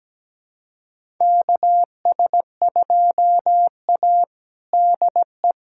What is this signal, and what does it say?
Morse code, heard as a keyed tone, about 17 wpm, 700 Hz. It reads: KS2A DE
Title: USA Callsign DE Callsign - 4 Characters - 17wpm